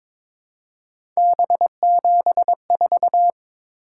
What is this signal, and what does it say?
Morse code, heard as a keyed tone, about 22 wpm, 700 Hz. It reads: B74
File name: practice / 22wpm / Three Letter-Number Combinations - 22wpm